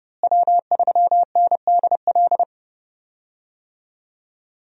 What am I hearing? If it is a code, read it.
Morse code, heard as a keyed tone, about 30 wpm, 700 Hz. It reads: W3NDL
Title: USA Call Signs - 5 Characters - Repeated 3x - 30wpm